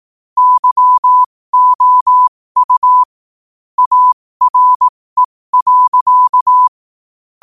Morse code, heard as a keyed tone, about 18 wpm, spaced as for 15 wpm, 1000 Hz. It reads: YOU ARE.
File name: qotdfiles / 15wpm